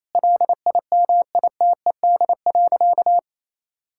Morse code, heard as a keyed tone, about 28 wpm, 700 Hz. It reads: LIMITED.
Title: Sentences from Top 1000 Words - Review Part 6 - 28wpm